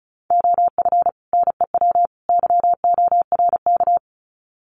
Morse code, heard as a keyed tone, about 34 wpm, 700 Hz. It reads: OF NEW YORK
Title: Sets of 3 Words 35wpm